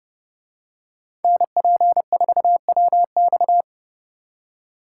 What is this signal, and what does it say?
Morse code, heard as a keyed tone, about 34 wpm, 700 Hz. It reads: NP4WX
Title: Speed Racing - USA 2x2 Call Signs 30wpm